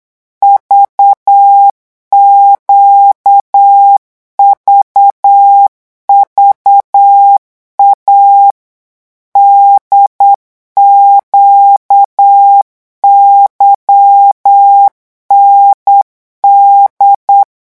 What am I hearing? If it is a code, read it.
Morse code, heard as a keyed tone, about 8 wpm, 800 Hz. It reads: VQVVA DQYND
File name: morse mp3 / revision3-vitesse 8 mots